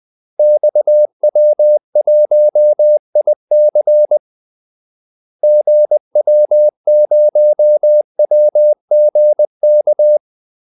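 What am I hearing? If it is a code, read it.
Morse code, heard as a keyed tone, about 20 wpm, 600 Hz. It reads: XW1IC GW0WGK
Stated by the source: Morse call-sign practice at 20 wpm.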